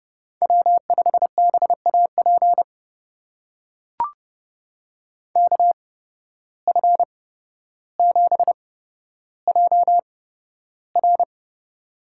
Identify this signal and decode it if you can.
Morse code, heard as a keyed tone, about 30 wpm, spaced as for 13 wpm, 700 Hz. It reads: W5BAP KF7JR
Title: USA Call Signs - 5 Characters - 8x Character Spacing - 30wpm